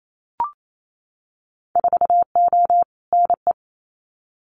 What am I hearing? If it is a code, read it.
Morse code, heard as a keyed tone, about 28 wpm, 700 Hz. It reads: 4O NE
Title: Winter Field Day Exchanges Encore 28wpm